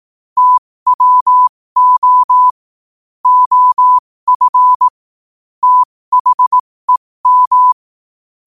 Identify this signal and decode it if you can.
Morse code, heard as a keyed tone, about 18 wpm, spaced as for 15 wpm, 1000 Hz. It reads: TWO OF THEM
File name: qotdfiles / 15wpm